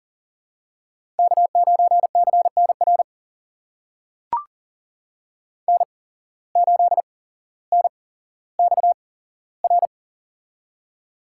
Morse code, heard as a keyed tone, about 42 wpm, spaced as for 17 wpm, 700 Hz. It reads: K9CNR N8NXR
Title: USA Call Signs - 5 Characters - 8x Character Spacing - 40wpm